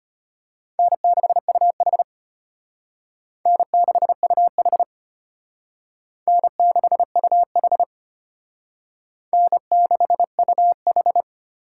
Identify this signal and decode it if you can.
Morse code, heard as a keyed tone, about 32 wpm, 700 Hz. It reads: N6UH N6UH N6UH N6UH